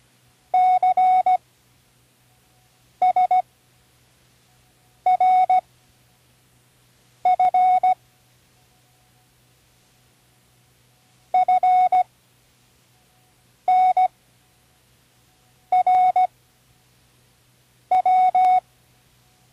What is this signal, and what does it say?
Morse code, heard as a keyed tone, about 16 wpm, spaced as for 5 wpm, 740 Hz. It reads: CSRF FNRW
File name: CWResources / Learning Morse Code (from Tune in the World) / 19 random